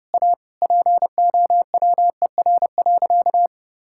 Morse code, heard as a keyed tone, about 32 wpm, 700 Hz. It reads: A POWER.
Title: Sentences from Top 1000 Words - Review Part 5 - 30wpm